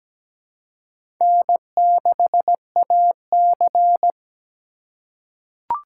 Morse code, heard as a keyed tone, about 17 wpm, 700 Hz. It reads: N6AC
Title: USA Call Signs - 4 Characters - 4x Character Spacing - 17wpm